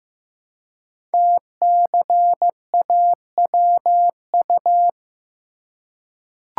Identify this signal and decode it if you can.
Morse code, heard as a keyed tone, about 15 wpm, 700 Hz. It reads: TCAWU
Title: Code Groups - 5 Letter:Numbers - 15wpm